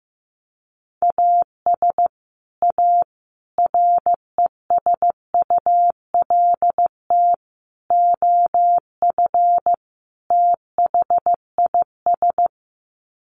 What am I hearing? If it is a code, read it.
Morse code, heard as a keyed tone, about 15 wpm, 700 Hz. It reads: AS A RESULT OF THIS
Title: Sets of 5 Words 15wpm